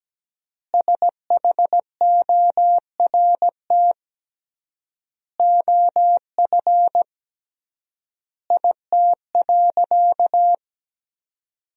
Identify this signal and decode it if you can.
Morse code, heard as a keyed tone, about 17 wpm, 700 Hz. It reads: SHORT OF IT.
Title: 3x Word-Spacing - Sentences from Top 400 Words - 17wpm